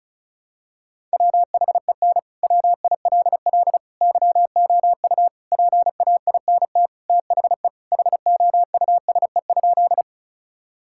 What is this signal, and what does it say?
Morse code, heard as a keyed tone, about 36 wpm, 700 Hz. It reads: WHEN WILL YOU PAINT THE HOUSE?
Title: Sentences from Top 1000 Words - Review Part 6 - 35wpm